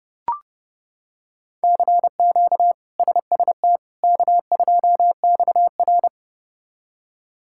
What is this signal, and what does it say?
Morse code, heard as a keyed tone, about 30 wpm, 700 Hz. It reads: CQ SST K2XR
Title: K1USN SST QSO Practice - 30wpm